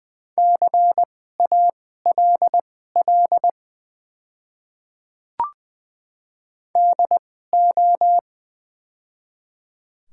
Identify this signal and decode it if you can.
Morse code, heard as a keyed tone, about 20 wpm, spaced as for 14 wpm, 700 Hz. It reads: CALL DO
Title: 2x Character Spacing - Top 100 Words - 20wpm